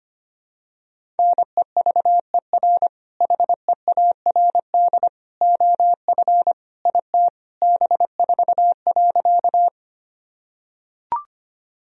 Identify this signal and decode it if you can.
Morse code, heard as a keyed tone, about 25 wpm, 700 Hz. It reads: NEVER HEARD OF IT B4.
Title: Sentences from Top 400 Words and Abbreviations 25wpm